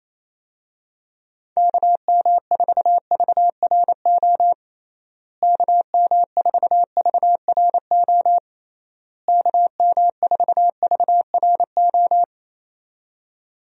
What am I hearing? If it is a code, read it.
Morse code, heard as a keyed tone, about 28 wpm, 700 Hz. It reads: KM4VRO KM4VRO KM4VRO
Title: USA Call Signs - 6 Characters - Repeated 3x - 28wpm